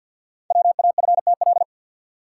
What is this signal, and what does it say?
Morse code, heard as a keyed tone, about 47 wpm, 690 Hz. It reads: WAFTL
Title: Code Groups - 5 Letters - 50wpm